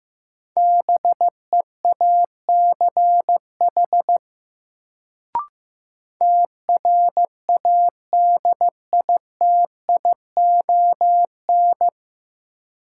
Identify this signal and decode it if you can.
Morse code, heard as a keyed tone, about 15 wpm, 700 Hz. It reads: BEACH TRADITION
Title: Sentences from Top 1500 Words - New Word Drill - 15wpm